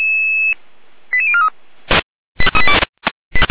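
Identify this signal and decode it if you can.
Five-tone selective call (ZVEI), heard as quick runs of five tones, tones of about 70 ms. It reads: 88043
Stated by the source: ZVEI five-tone call received on SDR.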